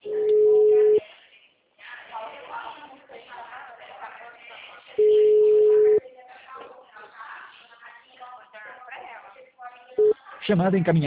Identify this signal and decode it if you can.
Ringback tone (Europe, 425 Hz), with speech.